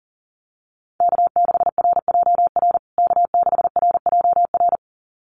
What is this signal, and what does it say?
Morse code, heard as a keyed tone, about 40 wpm, 700 Hz. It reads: K6RJR K6RJR
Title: USA Call Signs Repeated 2x 40wpm